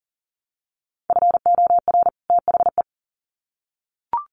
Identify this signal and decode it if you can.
Morse code, heard as a keyed tone, about 40 wpm, 700 Hz. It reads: FOR THE
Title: Sets of 2 Words 40wpm